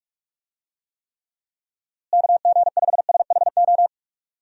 Morse code, heard as a keyed tone, about 46 wpm, 700 Hz. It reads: KG5SHO
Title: USA Call Signs - 6 Characters - Repeated 2x - 45wpm